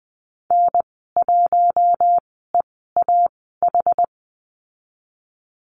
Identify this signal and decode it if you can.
Morse code, heard as a keyed tone, about 20 wpm, spaced as for 14 wpm, 700 Hz. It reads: N1EAH
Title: 062 Lesson 11g - Letter h - Call Signs 20wpm